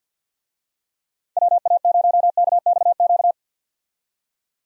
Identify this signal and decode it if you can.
Morse code, heard as a keyed tone, about 51 wpm, 700 Hz. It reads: WA0KXQ
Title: USA Call Signs - 6 Characters - 50wpm